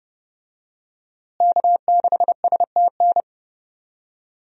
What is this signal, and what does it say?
Morse code, heard as a keyed tone, about 30 wpm, 700 Hz. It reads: K6STN
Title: USA Call Signs - 5 Characters - Repeated 2x - 30wpm